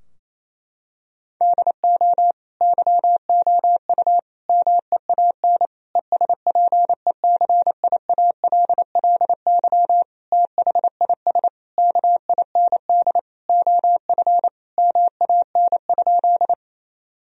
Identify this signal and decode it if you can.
Morse code, heard as a keyed tone, about 28 wpm, 700 Hz. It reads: DO YOU MEAN ESPECIALLY THIS KIND OF MAN?
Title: Sentences from Top 1000 Words - Review Part 4 - 28wpm